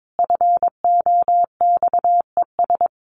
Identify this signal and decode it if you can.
Morse code, heard as a keyed tone, about 22 wpm, 700 Hz. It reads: FOXES
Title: A Sound Of Thunder 22wpm